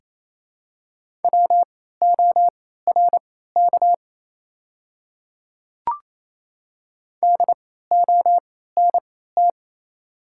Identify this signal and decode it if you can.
Morse code, heard as a keyed tone, about 28 wpm, spaced as for 16 wpm, 700 Hz. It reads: WORK DONT
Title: 3x Character Spacing - Top 300 Words - 28wpm